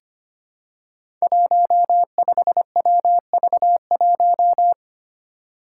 Morse code, heard as a keyed tone, about 25 wpm, 700 Hz. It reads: 15WV1
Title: Five Challenging Letter-Number Combinations 25wpm